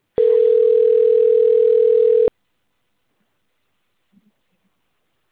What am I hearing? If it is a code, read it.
Ringback tone (North America), 440 + 480 Hz.